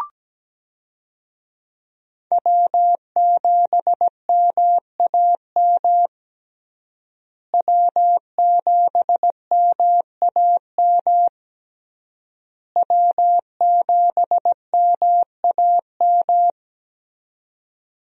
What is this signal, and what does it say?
Morse code, heard as a keyed tone, about 17 wpm, 700 Hz. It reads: W7MAM W7MAM W7MAM
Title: USA Call Signs - 5 Characters - Repeated 3x - 17wpm